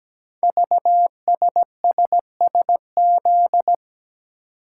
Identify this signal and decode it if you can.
Morse code, heard as a keyed tone, about 17 wpm, 700 Hz. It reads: VSSSZ
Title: Code Groups - 5 Letters - 17wpm